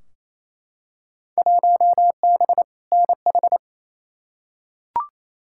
Morse code, heard as a keyed tone, about 28 wpm, 700 Hz. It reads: 1B NH
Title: Field Day Exchanges Encore - 28wpm